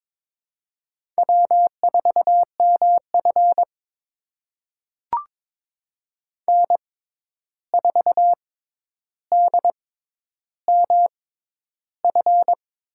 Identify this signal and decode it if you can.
Morse code, heard as a keyed tone, about 22 wpm, spaced as for 11 wpm, 700 Hz. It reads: W4MF N4DMF
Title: USA Call Signs - 6x Character Spacing - 22wpm